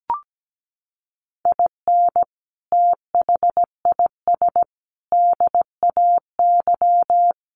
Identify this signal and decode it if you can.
Morse code, heard as a keyed tone, about 17 wpm, 700 Hz. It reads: IN THIS DAY
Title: Sets of 5 Words Encore 17wpm